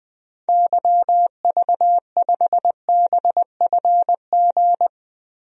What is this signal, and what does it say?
Morse code, heard as a keyed tone, about 21 wpm, 700 Hz. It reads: YV5BFG
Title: DXCC-100 Call Signs - 6 Characters - 20wpm